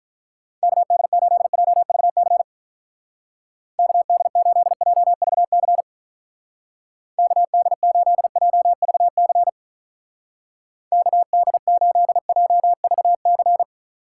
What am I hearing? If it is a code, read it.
Morse code, heard as a keyed tone, about 44 wpm, 700 Hz. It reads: KD8JVC KD8JVC KD8JVC KD8JVC
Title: Speed Racing - USA Call Signs 35wpm